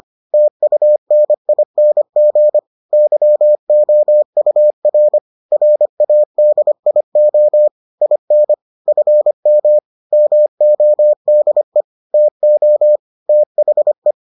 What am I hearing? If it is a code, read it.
Morse code, heard as a keyed tone, about 25 wpm, 600 Hz. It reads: TUNING YOUR RADIO IN FM MODE TO THE